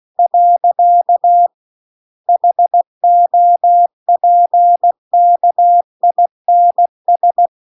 Morse code, heard as a keyed tone, about 16 wpm, 700 Hz. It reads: . HOPKINS